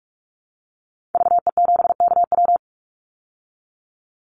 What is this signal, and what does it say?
Morse code, heard as a keyed tone, about 44 wpm, 700 Hz. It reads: VE7KW
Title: Call Signs Repeated 2x - 45wpm